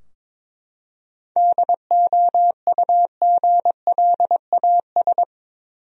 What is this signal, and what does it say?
Morse code, heard as a keyed tone, about 22 wpm, 700 Hz. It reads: DOUGLAS
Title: USA Common Names - 22wpm